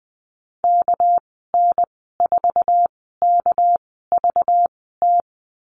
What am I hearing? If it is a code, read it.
Morse code, heard as a keyed tone, about 20 wpm, spaced as for 15 wpm, 700 Hz. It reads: KN4KVT
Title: 207 Lesson 33g - Letter j - Call Signs 20wpm